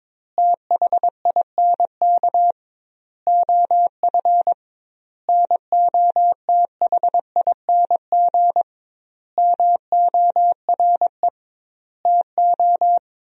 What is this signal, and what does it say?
Morse code, heard as a keyed tone, about 22 wpm, 700 Hz. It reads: THINK OF NOTHING MORE TO
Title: 2x Word-Spacing - Sentences from Top 500 Words - 22wpm